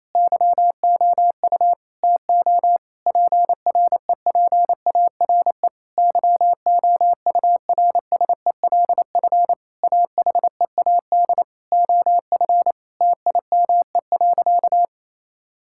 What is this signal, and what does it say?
Morse code, heard as a keyed tone, about 28 wpm, 700 Hz. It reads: YOU TO PREPARE YOURSELF AHEAD OF TIME.
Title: Sentences from Top 1000 Words - Review Part 6 - 28wpm